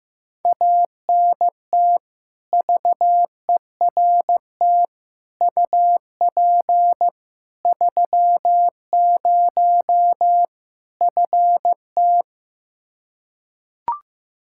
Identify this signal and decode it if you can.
Morse code, heard as a keyed tone, about 15 wpm, 700 Hz. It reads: ANT VERT UP 30 FT
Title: K1USN SST QSO Practice 15wpm